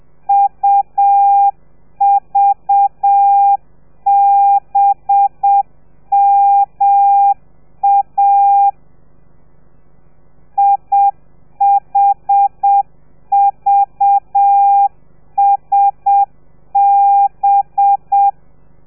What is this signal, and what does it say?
Morse code, heard as a keyed tone, about 7 wpm, 800 Hz. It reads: UVBMA IHVSB
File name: telegrafia / i2tfj / Lez4 30